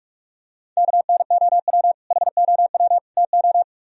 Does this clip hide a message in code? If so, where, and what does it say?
Yes: Morse code from 0.8 to 3.6 s, heard as a keyed tone, about 44 wpm, 700 Hz. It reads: KNOW HOW TO